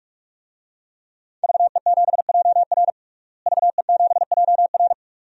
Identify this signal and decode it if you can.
Morse code, heard as a keyed tone, about 41 wpm, 700 Hz. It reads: VE7JR VE7JR
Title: DXCC-100 Call Signs - 5 Characters - Repeated 2x - 45wpm